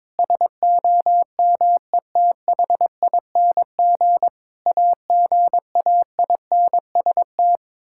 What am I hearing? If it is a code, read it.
Morse code, heard as a keyed tone, about 22 wpm, 700 Hz. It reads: SOMETHING AGAINST
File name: Sentences with 9-Word Phrases - 22wpm